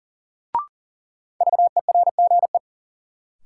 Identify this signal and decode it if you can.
Morse code, heard as a keyed tone, about 40 wpm, 700 Hz. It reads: VERGE
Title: Sets of 4 Words - New Word Drill - 40wpm